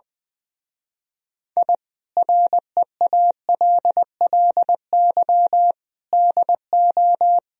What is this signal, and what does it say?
Morse code, heard as a keyed tone, about 22 wpm, 700 Hz. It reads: I REALLY DO
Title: Speed-Racing - Sets of 4 Words 20wpm